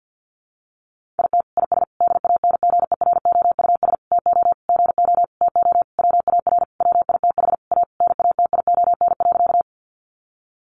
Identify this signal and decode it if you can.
Morse code, heard as a keyed tone, about 50 wpm, 700 Hz. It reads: IT IS DANGEROUS TO GO TO WAR WITH A NATION.
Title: Sentences from Top 1200 Words 50wpm